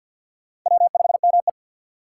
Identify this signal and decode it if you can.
Morse code, heard as a keyed tone, about 46 wpm, 700 Hz. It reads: W5ME
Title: USA Call Signs - 4 Characters - 3x Character Spacing - 50wpm